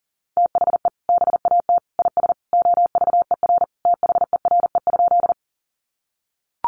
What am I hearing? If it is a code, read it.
Morse code, heard as a keyed tone, about 40 wpm, 700 Hz. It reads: THE BAT IS OVER THERE?
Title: Sentences from Top 1500 Words 40wpm